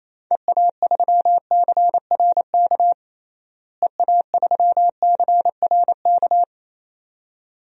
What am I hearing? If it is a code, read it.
Morse code, heard as a keyed tone, about 27 wpm, 700 Hz. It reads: EA3CRK EA3CRK
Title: DXCC-100 Call Signs - 6 Characters - Repeated 2x - 28wpm